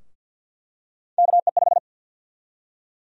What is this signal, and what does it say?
Morse code, heard as a keyed tone, about 53 wpm, 700 Hz. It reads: KE5
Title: USA Call Sign Prefixes - Repeated 2x - 50wpm